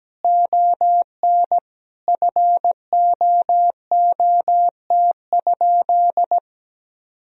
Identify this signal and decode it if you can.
Morse code, heard as a keyed tone, about 17 wpm, 700 Hz. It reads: ON FOOT?
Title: Sentences from Top 1000 Words - Review Part 3 - 17wpm